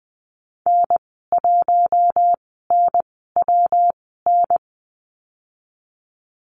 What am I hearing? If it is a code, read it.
Morse code, heard as a keyed tone, about 20 wpm, spaced as for 14 wpm, 700 Hz. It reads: N1NWN
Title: 122 Lesson 20g - Number 3 - Call Signs 20wpm